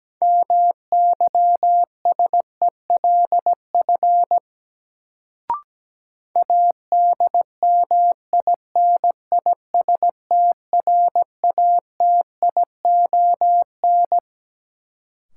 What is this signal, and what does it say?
Morse code, heard as a keyed tone, about 17 wpm, 700 Hz. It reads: MYSELF ADMINISTRATION
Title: Sentences from Top 1300 Words - New Word Drill Alt - 17wpm